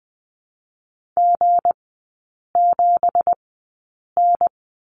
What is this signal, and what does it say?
Morse code, heard as a keyed tone, about 20 wpm, spaced as for 8 wpm, 690 Hz. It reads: G7N
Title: 166 Lesson 27e - Number 7 - Three Characters 20wpm